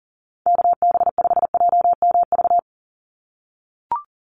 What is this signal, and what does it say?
Morse code, heard as a keyed tone, about 40 wpm, 700 Hz. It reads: KB5JMV
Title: USA Call Signs 40wpm